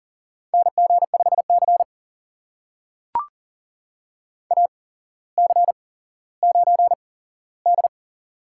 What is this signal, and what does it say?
Morse code, heard as a keyed tone, about 39 wpm, spaced as for 17 wpm, 700 Hz. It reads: NG5C AC9D
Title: USA Call Signs - 4 Characters - 8x Character Spacing - 40wpm